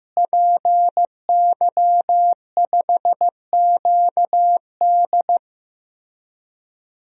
Morse code, heard as a keyed tone, about 15 wpm, 700 Hz. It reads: PY5QD
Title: DXCC-100 Call Signs - 5 Characters - Repeated 3x - 15wpm